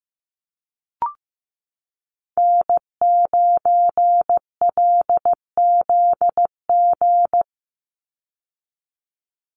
Morse code, heard as a keyed tone, about 15 wpm, 700 Hz. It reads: N9LZG
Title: USA Call Signs 15wpm